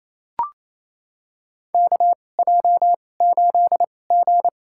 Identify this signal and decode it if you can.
Morse code, heard as a keyed tone, about 27 wpm, spaced as for 20 wpm, 700 Hz. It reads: KJ8G
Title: USA Call Signs - 2x Character Spacing - 28wpm